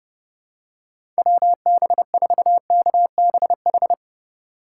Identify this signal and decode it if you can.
Morse code, heard as a keyed tone, about 32 wpm, 700 Hz. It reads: WB4KBH
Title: USA Call Signs - 3x Character Spacing - 30wpm